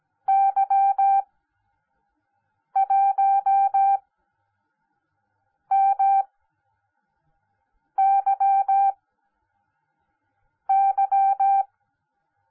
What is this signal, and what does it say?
Morse code, heard as a keyed tone, about 17 wpm, spaced as for 5 wpm, 780 Hz. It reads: Y1MYY
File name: Onlinelehrgang / morsen / DJ4UF Morsen10